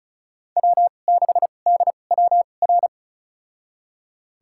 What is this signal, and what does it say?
Morse code, heard as a keyed tone, about 34 wpm, spaced as for 25 wpm, 700 Hz. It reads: W6DWR
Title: USA Call Signs - 5 Characters - 2x Character Spacing - 35wpm